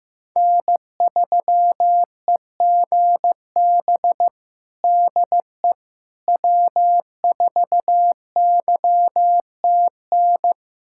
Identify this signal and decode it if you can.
Morse code, heard as a keyed tone, about 15 wpm, 700 Hz. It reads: N3EGB DE W4YTN
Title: USA Callsign DE Callsign - 5 Characters - 15wpm